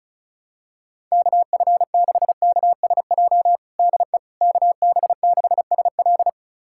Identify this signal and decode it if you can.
Morse code, heard as a keyed tone, about 36 wpm, 700 Hz. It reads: KF6KSJ DE KB6SL
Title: USA Callsign DE Callsign 35wpm